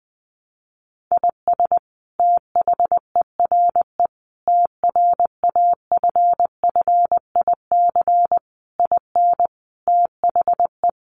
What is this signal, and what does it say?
Morse code, heard as a keyed tone, about 20 wpm, 700 Hz. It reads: IS THERE TRAFFIC IN THE
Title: Sentences from Top 1300 Words 20wpm